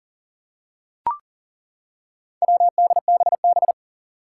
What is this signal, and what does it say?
Morse code, heard as a keyed tone, about 38 wpm, 700 Hz. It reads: WDBB
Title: Four Challenging Letter-Number Combinations 40wpm